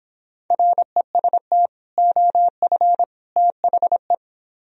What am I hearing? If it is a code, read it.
Morse code, heard as a keyed tone, about 26 wpm, 700 Hz. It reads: REST OF THE